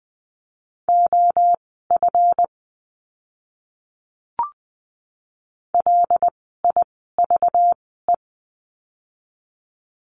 Morse code, heard as a keyed tone, about 20 wpm, spaced as for 15 wpm, 700 Hz. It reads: OF LIVE
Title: 237 Lesson 38f - Letter q - Words 20wpm